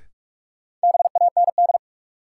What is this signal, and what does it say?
Morse code, heard as a keyed tone, about 42 wpm, 700 Hz. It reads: BAND
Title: Sentences from Top 1600 Words - New Word Drill Alt - 45wpm